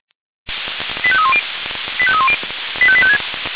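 Five-tone selective call (ZVEI), heard as quick runs of five tones, tones of about 70 ms. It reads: 85310 85310 85856
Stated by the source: ZVEI five-tone call 85856 received by SDR.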